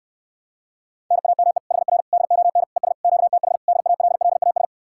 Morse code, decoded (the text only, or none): NAME HR DON ES QTH DALLAS